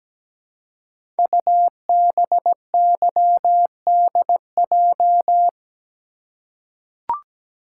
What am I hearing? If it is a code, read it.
Morse code, heard as a keyed tone, about 17 wpm, 700 Hz. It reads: UBYDJ
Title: Code Groups - 5 Letters - 17wpm